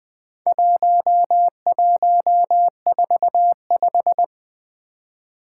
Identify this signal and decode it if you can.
Morse code, heard as a keyed tone, about 20 wpm, 700 Hz. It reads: 1145
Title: Four Challenging Letter-Number Combinations 20wpm